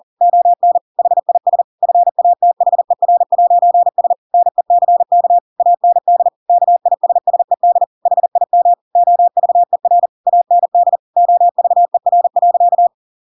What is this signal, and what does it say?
Morse code, heard as a keyed tone, about 40 wpm, 700 Hz. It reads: ON HIS FATHER'S NECK AND KISSED HIM OVER AND OVER.
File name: Pinocchio / Pinocchio ch8 40 40 ik1hge